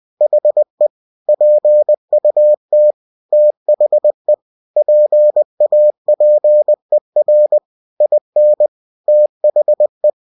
Morse code, heard as a keyed tone, about 20 wpm, 600 Hz. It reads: HE PUT THE PAPER IN THE
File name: cw-books / winnie-the-pooh / winnie-20wpm-page-0092